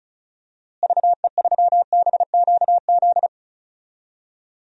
Morse code, heard as a keyed tone, about 36 wpm, 700 Hz. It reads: VE3BQZ
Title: DXCC-100 Call Signs - 6 Characters - Repeated 3x - 35wpm